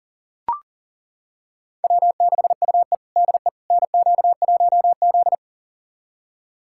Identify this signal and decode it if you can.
Morse code, heard as a keyed tone, about 40 wpm, 700 Hz. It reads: W6UE DE NQ1Z
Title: USA Callsign DE Callsign - 4 Characters - 40wpm